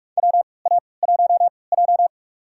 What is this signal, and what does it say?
Morse code, heard as a keyed tone, about 46 wpm, spaced as for 25 wpm, 700 Hz. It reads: WA1J
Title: USA Call Signs - 4 Characters - 3x Character Spacing - 45wpm